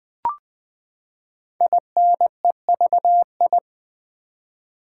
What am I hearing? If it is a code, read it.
Morse code, heard as a keyed tone, about 20 wpm, 700 Hz. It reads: INEVI
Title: Build A Word - 5 Letters At A Time - 10-Letter Words - 20wpm